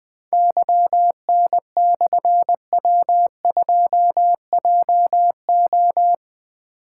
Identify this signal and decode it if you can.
Morse code, heard as a keyed tone, about 20 wpm, 700 Hz. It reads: YN/W2JO
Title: Difficult Call Signs 20wpm